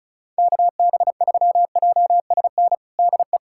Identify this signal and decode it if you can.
Morse code, heard as a keyed tone, about 36 wpm, 690 Hz. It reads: KB3JSN DE
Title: USA Callsign DE Callsign - 6 Characters - 35wpm